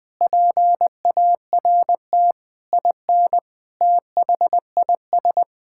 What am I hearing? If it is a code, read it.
Morse code, heard as a keyed tone, about 20 wpm, 700 Hz. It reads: PART IN THIS
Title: Sentences from Top 1000 Words - Review Part 5 - 20wpm